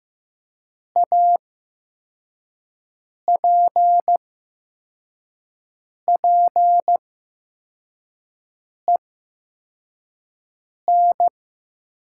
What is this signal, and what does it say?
Morse code, heard as a keyed tone, about 15 wpm, spaced as for 4 wpm, 700 Hz. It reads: APPEN
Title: 8x Character Spacing - Top 500 Words - 15wpm